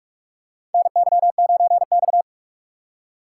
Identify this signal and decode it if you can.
Morse code, heard as a keyed tone, about 46 wpm, 700 Hz. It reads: NY9X
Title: K1USN SST QSO Practice - 45wpm